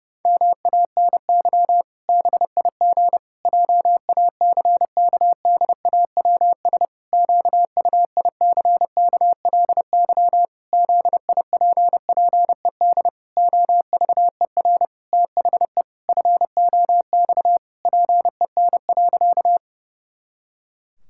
Morse code, heard as a keyed tone, about 28 wpm, 700 Hz. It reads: MANY BIG JACKDAWS QUICKLY ZIPPED OVER THE FOX PEN.
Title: Pangram Challenge v2 - 30wpm